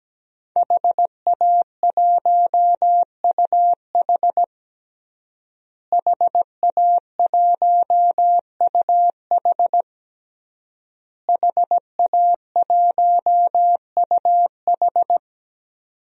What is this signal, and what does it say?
Morse code, heard as a keyed tone, about 17 wpm, 700 Hz. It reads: HA1UH HA1UH HA1UH
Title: DXCC-100 Call Signs - 5 Characters - Repeated 3x - 17wpm